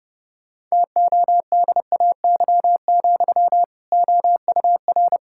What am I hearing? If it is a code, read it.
Morse code, heard as a keyed tone, about 30 wpm, 700 Hz. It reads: TODAY, OUR
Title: Sentences from Top 1600 Words 30wpm